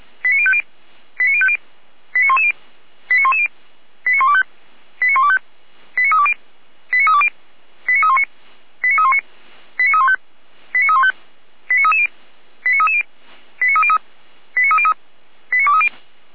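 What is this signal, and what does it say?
Five-tone selective call (ZVEI), heard as quick runs of five tones, tones of about 70 ms. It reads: 89050 89050 89110 89110 89125 89125 89320 89320 89319 89319 89315 89315 89330 89330 89303 89303 89230
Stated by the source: ZVEI five-tone selective call received on an SDR.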